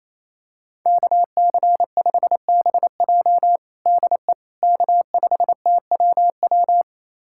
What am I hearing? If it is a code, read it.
Morse code, heard as a keyed tone, about 28 wpm, 700 Hz. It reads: KC5BJ DE K5TWW
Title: USA Callsign DE Callsign - 5 Characters - 28wpm